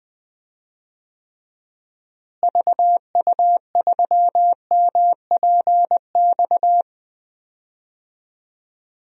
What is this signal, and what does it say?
Morse code, heard as a keyed tone, about 20 wpm, 700 Hz. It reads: VU3MPX
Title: DXCC-100 Call Signs - 6 Characters - Repeated 2x - 20wpm